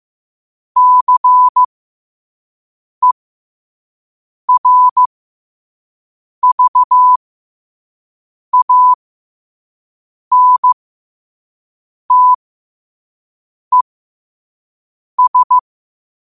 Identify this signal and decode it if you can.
Morse code, heard as a keyed tone, about 15 wpm, spaced as for 5 wpm, 1000 Hz. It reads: CERVANTES
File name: qotdfiles / 5wpm